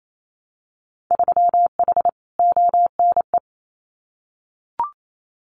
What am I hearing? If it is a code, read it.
Morse code, heard as a keyed tone, about 28 wpm, 700 Hz. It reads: 3H ONE
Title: Winter Field Day Exchanges Encore 28wpm